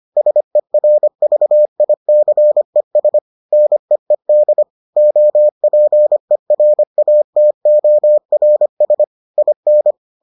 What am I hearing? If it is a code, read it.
Morse code, heard as a keyed tone, about 25 wpm, 600 Hz. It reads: SERVICES NEED OPERATORS IN